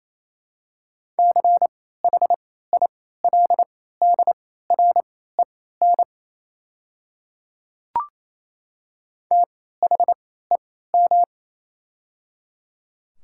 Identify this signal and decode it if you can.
Morse code, heard as a keyed tone, about 28 wpm, spaced as for 16 wpm, 700 Hz. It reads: CHILDREN THEM
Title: 3x Character Spacing - Top 300 Words - 28wpm